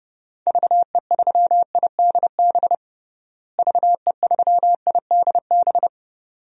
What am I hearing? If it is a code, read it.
Morse code, heard as a keyed tone, about 31 wpm, 700 Hz. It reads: VE3IDB VE3IDB
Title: DXCC-100 Call Signs - 6 Characters - Repeated 2x - 30wpm